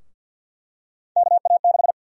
Morse code, heard as a keyed tone, about 51 wpm, 700 Hz. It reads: KA6
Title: USA Call Sign Prefixes - Repeated 2x - 50wpm